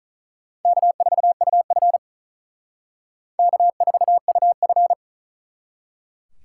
Morse code, heard as a keyed tone, about 39 wpm, 700 Hz. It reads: K4UF K4UF
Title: Speed Racing - USA 1x2 Call Signs 35wpm